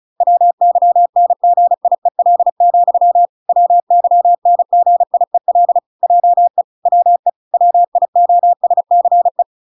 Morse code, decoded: WYNGIEL, WYNGIEL JE WE WIOSCE

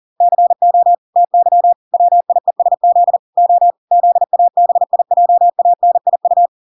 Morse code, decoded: CO TY WIESZ O ZABIJANIU

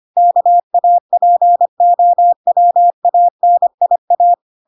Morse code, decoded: KAPOWANIA

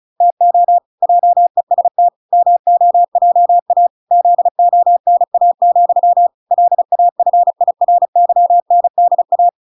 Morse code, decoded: TO JEST MOJA ZONA, LAFIRYNDA

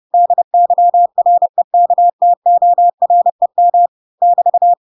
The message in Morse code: DYREKTOREM <BT>

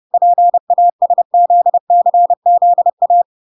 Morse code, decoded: PASZCZA

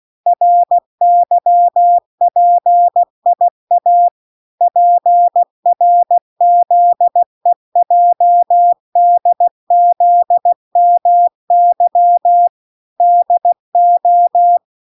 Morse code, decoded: RYPIA PRZEJDZMY DO

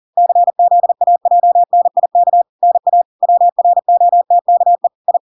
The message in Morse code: CZAJNIK NA WROTKE I

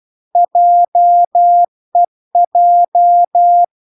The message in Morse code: JEJ

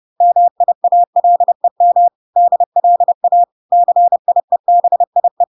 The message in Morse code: MIALEM DLA CIEBIE

